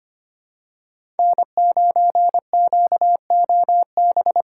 Morse code, heard as a keyed tone, about 25 wpm, 700 Hz. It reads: N9QOB
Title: USA Call Signs - 2x Character Spacing - 25wpm